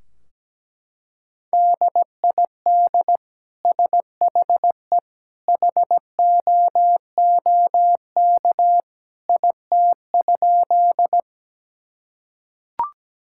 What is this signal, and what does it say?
Morse code, heard as a keyed tone, about 17 wpm, 700 Hz. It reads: DID SHE HOOK IT?
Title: Sentences from Top 1600 Words 17wpm